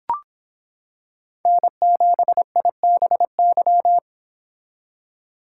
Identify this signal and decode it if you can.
Morse code, heard as a keyed tone, about 26 wpm, 700 Hz. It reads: N7IBY